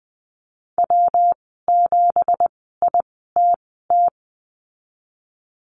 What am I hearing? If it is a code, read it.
Morse code, heard as a keyed tone, about 20 wpm, spaced as for 15 wpm, 700 Hz. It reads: W7ITT
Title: 167 Lesson 27g - Number 7 - Call Signs 20wpm